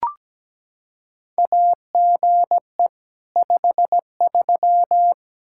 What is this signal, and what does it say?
Morse code, heard as a keyed tone, about 17 wpm, 700 Hz. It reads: AGE 53
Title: K1USN SST QSO Practice 17wpm